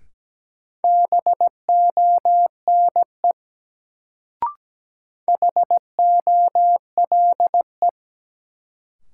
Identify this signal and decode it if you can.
Morse code, heard as a keyed tone, about 17 wpm, 700 Hz. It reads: BONE HOLE
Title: Sentences from Top 1600 Words - New Word Drill Alt - 17wpm